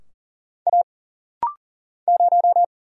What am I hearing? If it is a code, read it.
Morse code, heard as a keyed tone, about 41 wpm, 700 Hz. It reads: A 0
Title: Single Letter-Number Alt - Rapid-Fire 40wpm